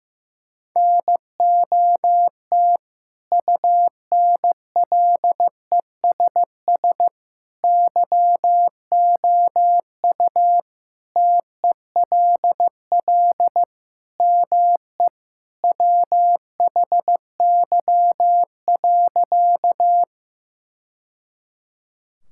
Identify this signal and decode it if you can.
Morse code, heard as a keyed tone, about 15 wpm, 700 Hz. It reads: NOT UNLESS YOU TELL ME WHY.
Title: Sentences from Top 1000 Words - Review Part 6 - 15wpm